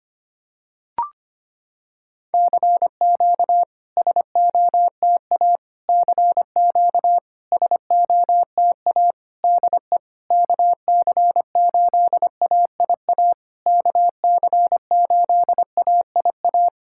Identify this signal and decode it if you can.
Morse code, heard as a keyed tone, about 25 wpm, 700 Hz. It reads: CQ SOTA CQ SOTA DE KC8AIA KC8AIA
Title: SOTA QSO Practice 25wpm